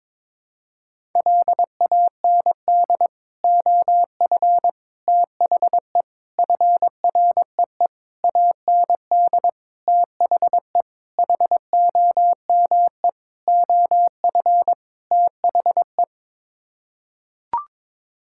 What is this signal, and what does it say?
Morse code, heard as a keyed tone, about 22 wpm, 700 Hz. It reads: LAND OF THE FREE AND THE HOME OF THE
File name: Sets of 9 Words - 22wpm